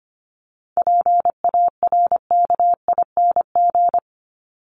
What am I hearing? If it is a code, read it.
Morse code, heard as a keyed tone, about 25 wpm, 700 Hz. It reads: PARKING
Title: Top 1501-1750 Words 25wpm